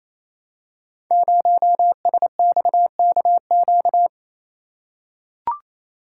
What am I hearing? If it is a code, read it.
Morse code, heard as a keyed tone, about 28 wpm, 700 Hz. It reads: 0SXKQ
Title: Code Groups - 5 Letter:Numbers - 28wpm